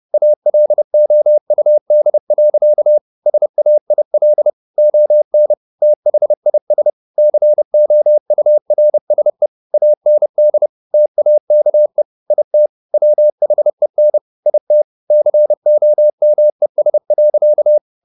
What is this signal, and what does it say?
Morse code, heard as a keyed tone, about 30 wpm, 600 Hz. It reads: ALOUD. SAIL ON THIS COURSE AND TAKE IT WHEN IT COMES.